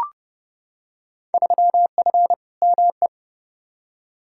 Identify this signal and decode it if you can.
Morse code, heard as a keyed tone, about 30 wpm, 700 Hz. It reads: 3F ME
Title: Field Day Exchanges - 30wpm